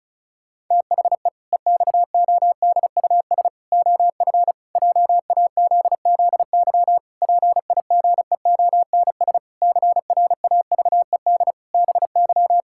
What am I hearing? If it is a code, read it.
Morse code, heard as a keyed tone, about 34 wpm, 700 Hz. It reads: THE EXODUS OF JAZZY PIGEONS CRAVED BY
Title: Pangram Challenge v2 - 35wpm